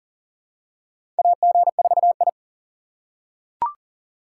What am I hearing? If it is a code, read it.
Morse code, heard as a keyed tone, about 40 wpm, 700 Hz. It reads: AG4I
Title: USA Call Signs - 4 Characters - 4x Character Spacing - 40wpm